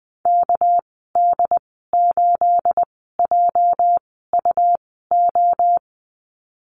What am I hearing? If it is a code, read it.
Morse code, heard as a keyed tone, about 20 wpm, spaced as for 14 wpm, 700 Hz. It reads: KD8JUO
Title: 219 Lesson 35g - Number 0 - Call Signs 20wpm